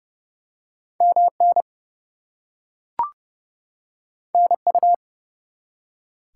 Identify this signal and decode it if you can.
Morse code, heard as a keyed tone, about 30 wpm, 700 Hz. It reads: MN NU
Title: US States and Canadian Provinces 30wpm